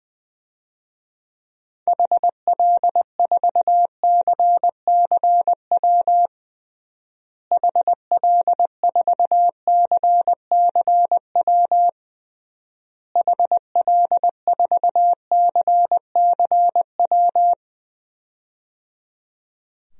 Morse code, heard as a keyed tone, about 20 wpm, 700 Hz. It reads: HL4CCW HL4CCW HL4CCW
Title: DXCC-100 Call Signs - 6 Characters - Repeated 3x - 20wpm